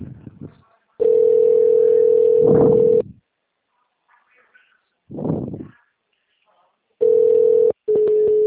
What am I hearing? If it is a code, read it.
Ringback tone (North America), 440 + 480 Hz.
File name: out- -2014- - - .75242